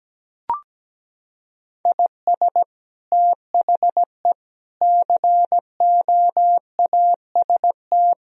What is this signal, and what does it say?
Morse code, heard as a keyed tone, about 17 wpm, 700 Hz. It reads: IS THE COAST
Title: Sentences from Top 1000 Words - Review Part 4 - 17wpm